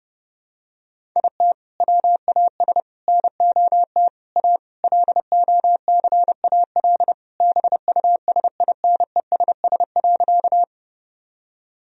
Morse code, heard as a keyed tone, about 30 wpm, 700 Hz. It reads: IT WAS NOT A LOCAL BUSINESS.
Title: Sentences from Top 1000 Words - Review Part 3 - 30wpm